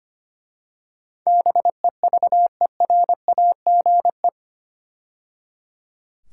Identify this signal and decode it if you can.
Morse code, heard as a keyed tone, about 25 wpm, 700 Hz. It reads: BEVERAGE
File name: 8 Letter Words - First 3 Letters Given - 25wpm